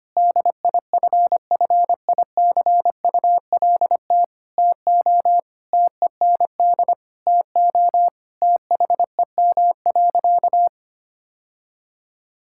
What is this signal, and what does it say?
Morse code, heard as a keyed tone, about 25 wpm, 700 Hz. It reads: DIFFICULT TO TEND TO THEM.
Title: Sentences from Top 1000 Words - Review Part 5 - 25wpm